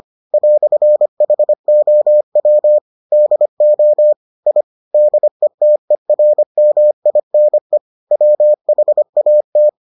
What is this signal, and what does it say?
Morse code, heard as a keyed tone, about 25 wpm, 600 Hz. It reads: "HOW DO I DETERMINE WHAT